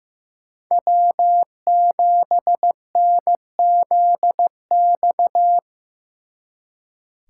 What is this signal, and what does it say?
Morse code, heard as a keyed tone, about 18 wpm, 700 Hz. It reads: W7NZX
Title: Speed Racing - USA Call Signs 15wpm